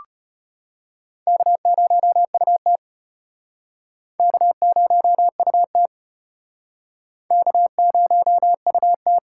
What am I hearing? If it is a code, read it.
Morse code, heard as a keyed tone, about 36 wpm, 700 Hz. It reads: K0UT K0UT K0UT